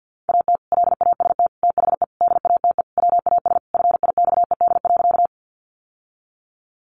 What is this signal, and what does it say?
Morse code, heard as a keyed tone, about 48 wpm, 700 Hz. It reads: AT LAST THE DATE WAS FIXED.
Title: Sentences from Top 1100 Words 50wpm